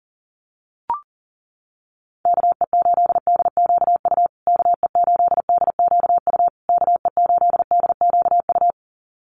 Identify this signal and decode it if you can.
Morse code, heard as a keyed tone, about 40 wpm, 700 Hz. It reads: KE8DQU KE8DQU KE8DQU
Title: USA Call Signs Repeated 3x 40wpm